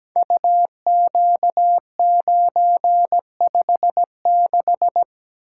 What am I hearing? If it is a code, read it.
Morse code, heard as a keyed tone, about 17 wpm, 700 Hz. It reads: UQ956
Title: Code Groups - 5 Letter:Numbers - 17wpm